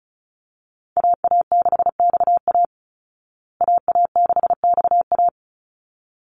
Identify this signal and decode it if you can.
Morse code, heard as a keyed tone, about 34 wpm, 700 Hz. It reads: AA6XA AA6XA
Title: Call Signs Repeated 3x - 35wpm